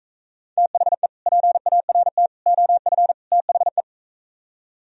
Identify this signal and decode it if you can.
Morse code, heard as a keyed tone, about 41 wpm, 700 Hz. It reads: THE PART OF THE